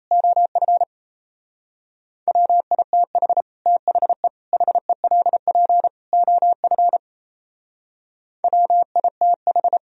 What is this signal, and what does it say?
Morse code, heard as a keyed tone, about 38 wpm, 700 Hz. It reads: OF WITH THE HELP OF WITH